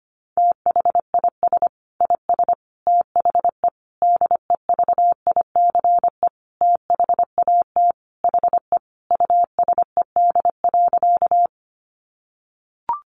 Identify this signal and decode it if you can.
Morse code, heard as a keyed tone, about 25 wpm, 700 Hz. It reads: THIS IS THE DEVICE THAT HE USED.
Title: Sentences from Top 1500 Words 25wpm